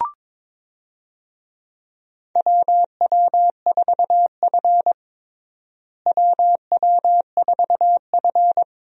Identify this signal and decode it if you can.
Morse code, heard as a keyed tone, about 22 wpm, 700 Hz. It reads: WW4F WW4F
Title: USA Call Signs - 4 Characters - Repeated 3x - 22wpm